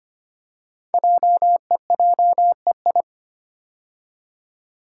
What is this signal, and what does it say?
Morse code, heard as a keyed tone, about 25 wpm, 700 Hz. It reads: JEJEI
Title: Code Groups - 5 Letter:Numbers - 25wpm